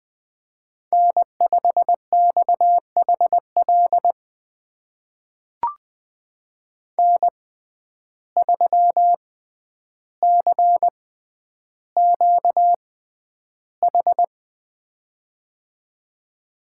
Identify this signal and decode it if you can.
Morse code, heard as a keyed tone, about 20 wpm, spaced as for 10 wpm, 700 Hz. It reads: N5XHL N3CQH
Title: USA Call Signs - 5 Characters - 6x Character Spacing - 20wpm